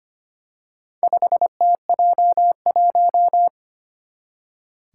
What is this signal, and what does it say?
Morse code, heard as a keyed tone, about 25 wpm, 700 Hz. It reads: 5TJ1
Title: Four Challenging Letter-Number Combinations 25wpm